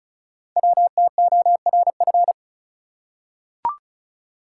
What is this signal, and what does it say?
Morse code, heard as a keyed tone, about 36 wpm, 700 Hz. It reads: WTORF
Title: Code Groups - 5 Letter:Numbers - 35wpm